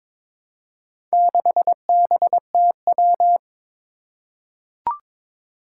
Morse code, heard as a keyed tone, about 22 wpm, 700 Hz. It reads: 6BTW
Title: Four Challenging Letter-Number Combinations 22wpm